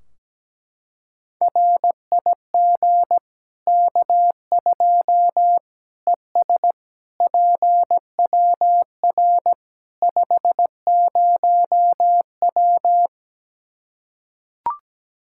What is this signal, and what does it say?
Morse code, heard as a keyed tone, about 17 wpm, 700 Hz. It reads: RIG K2 ES PWR 50W
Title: Practice QSO - Exchange 2 - Info - 17wpm